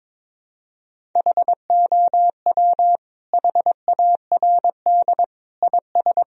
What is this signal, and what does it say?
Morse code, heard as a keyed tone, about 22 wpm, 700 Hz. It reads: HOW HARD IS
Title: Sentences from Top 1000 Words - Review Part 3 - 22wpm